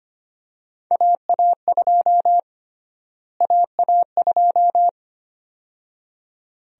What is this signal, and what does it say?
Morse code, heard as a keyed tone, about 25 wpm, 700 Hz. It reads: AA2 AA2
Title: USA Call Sign Prefixes - Repeated 2x - 25wpm